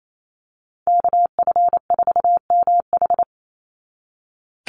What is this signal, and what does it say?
Morse code, heard as a keyed tone, about 28 wpm, 700 Hz. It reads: KF4MH
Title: NA SOTA Chaser Callsigns 28wpm